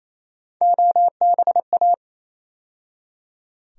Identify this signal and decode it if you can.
Morse code, heard as a keyed tone, about 27 wpm, 700 Hz. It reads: OBA
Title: USA Call Sign Postfixes - 3 Letters 28wpm